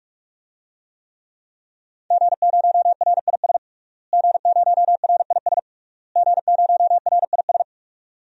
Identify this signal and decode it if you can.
Morse code, heard as a keyed tone, about 41 wpm, 700 Hz. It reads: G0RIS G0RIS G0RIS
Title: DXCC-100 Call Signs - 5 Characters - Repeated 3x - 45wpm